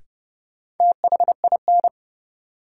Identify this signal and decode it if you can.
Morse code, heard as a keyed tone, about 30 wpm, 700 Hz. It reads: THIN
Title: Sentences from Top 1200 Words - New Word Drill Alt - 30wpm